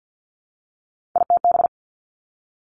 Morse code, heard as a keyed tone, about 50 wpm, 700 Hz. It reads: ITB
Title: Three Challenging Letter-Number Combinations 50wpm